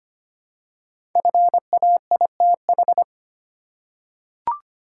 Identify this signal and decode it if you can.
Morse code, heard as a keyed tone, about 25 wpm, 700 Hz. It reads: FAITH
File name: 5 Letter Words - First 3 Letters Given - 25wpm